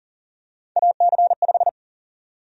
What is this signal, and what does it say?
Morse code, heard as a keyed tone, about 40 wpm, 700 Hz. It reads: AC5
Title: USA Call Sign Prefixes - 40wpm